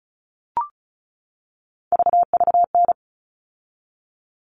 Morse code, heard as a keyed tone, about 34 wpm, 700 Hz. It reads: VVN
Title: All Cut Numbers - 3-Digit 35wpm